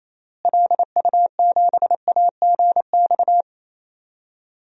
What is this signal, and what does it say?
Morse code, heard as a keyed tone, about 28 wpm, 700 Hz. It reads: LU7AGX
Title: DXCC-100 Call Signs - 6 Characters - Repeated 3x - 28wpm